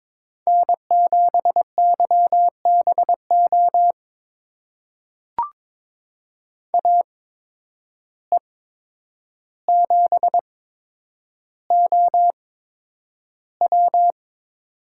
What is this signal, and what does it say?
Morse code, heard as a keyed tone, about 22 wpm, spaced as for 9 wpm, 700 Hz. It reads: N7YBO AE7OW
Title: USA Call Signs - 5 Characters - 8x Character Spacing - 22wpm